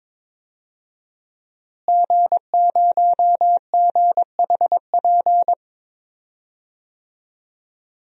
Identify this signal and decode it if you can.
Morse code, heard as a keyed tone, about 21 wpm, 700 Hz. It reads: G0GHP
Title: DXCC-100 Call Signs - 5 Characters - Repeated 2x - 22wpm